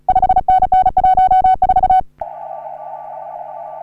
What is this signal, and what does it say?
Morse code, heard as a keyed tone, about 37 wpm, 750 Hz. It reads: 5NN 14
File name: CW / HA3OU RHOBCW REC